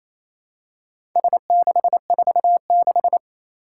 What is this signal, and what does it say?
Morse code, heard as a keyed tone, about 29 wpm, 700 Hz. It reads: S646
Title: Four Challenging Letter-Number Combinations 28wpm